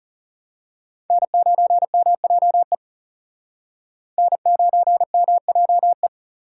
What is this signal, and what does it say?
Morse code, heard as a keyed tone, about 34 wpm, 700 Hz. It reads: N9MJE N9MJE
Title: Speed Racing - USA 1x3 Call Signs 30wpm